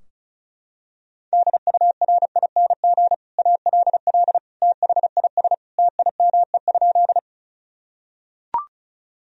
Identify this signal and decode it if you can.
Morse code, heard as a keyed tone, about 34 wpm, 700 Hz. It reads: DURING ALL THIS TIME?
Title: Sentences from Top 1000 Words - Review Part 2 - 35wpm